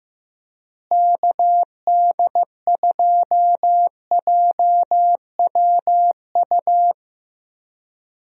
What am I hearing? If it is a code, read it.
Morse code, heard as a keyed tone, about 15 wpm, 700 Hz. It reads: KD2JWU
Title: USA Call Signs - 6 Characters - Repeated 3x - 15wpm